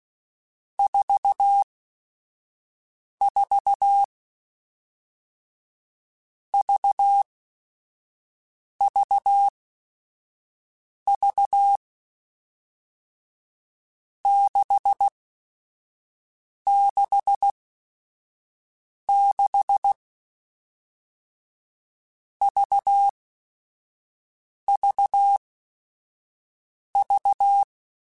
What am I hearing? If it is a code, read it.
Morse code, heard as a keyed tone, about 16 wpm, 800 Hz. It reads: 4 4 V V V 6 6 6 V V V
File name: Groups@5 / Groups@5 mp3s / VB46@5